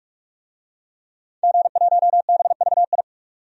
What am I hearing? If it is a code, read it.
Morse code, heard as a keyed tone, about 43 wpm, 700 Hz. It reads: G1BVI
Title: DXCC-100 Call Signs - 5 Characters - 45wpm